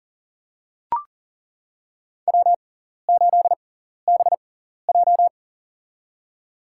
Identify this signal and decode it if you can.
Morse code, heard as a keyed tone, about 40 wpm, spaced as for 14 wpm, 700 Hz. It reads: W8BJ
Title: USA Call Signs - 6x Character Spacing - 40wpm